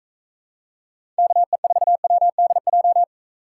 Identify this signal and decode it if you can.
Morse code, heard as a keyed tone, about 41 wpm, 700 Hz. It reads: KE4WDJ